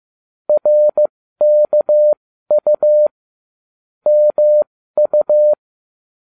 Morse code, heard as a keyed tone, about 15 wpm, spaced as for 13 wpm, 600 Hz. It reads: RKU MU